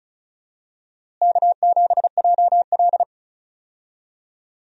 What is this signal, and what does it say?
Morse code, heard as a keyed tone, about 36 wpm, 700 Hz. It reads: K7JL
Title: USA Call Signs - 4 Characters - Repeated 2x - 35wpm